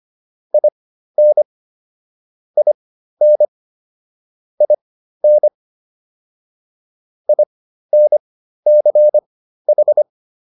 Morse code, heard as a keyed tone, about 25 wpm, spaced as for 13 wpm, 600 Hz. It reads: IN IN IN INCH